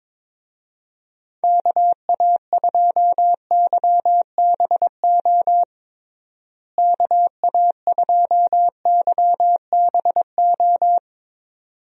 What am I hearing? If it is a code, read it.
Morse code, heard as a keyed tone, about 22 wpm, 700 Hz. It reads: KA2YBO KA2YBO
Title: USA Call Signs - 6 Characters - Repeated 2x - 22wpm